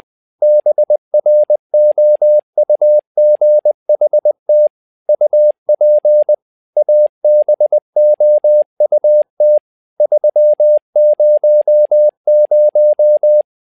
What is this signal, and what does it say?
Morse code, heard as a keyed tone, about 20 wpm, 600 Hz. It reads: BROUGHT UP ABOUT 300
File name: tower trouble 20wpm